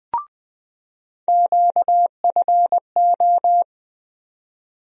Morse code, heard as a keyed tone, about 20 wpm, 700 Hz. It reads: QFO
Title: USA Call Sign Postfixes - 3 Letters 20wpm